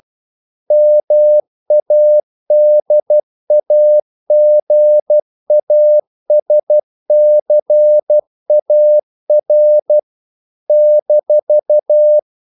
Morse code, heard as a keyed tone, about 12 wpm, 600 Hz. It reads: MADAGASCAR -
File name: CW / Practice / NationalCapitals12